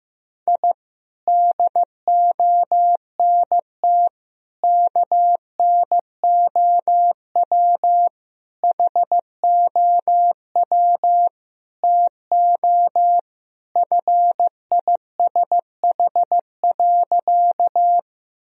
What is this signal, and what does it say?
Morse code, heard as a keyed tone, about 15 wpm, 700 Hz. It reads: I DONT KNOW HOW TO FISH.
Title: Sentences from Top 1000 Words - Review Part 1 - 15wpm